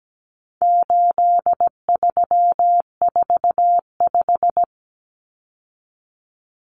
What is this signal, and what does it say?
Morse code, heard as a keyed tone, about 17 wpm, 700 Hz. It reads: 8345
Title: Common Cut Numbers - 4-Digit 17wpm